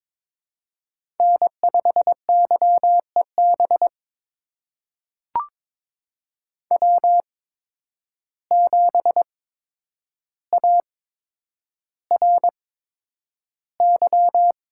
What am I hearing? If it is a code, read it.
Morse code, heard as a keyed tone, about 22 wpm, spaced as for 9 wpm, 700 Hz. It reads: N5YEB W7ARY
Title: USA Call Signs - 5 Characters - 8x Character Spacing - 22wpm